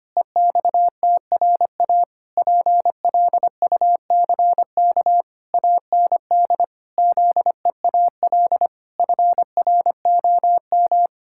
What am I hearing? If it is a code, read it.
Morse code, heard as a keyed tone, about 25 wpm, 700 Hz. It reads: EXTRA PLUCK AND ZEAL FROM
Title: Pangram Challenge v2 - 25wpm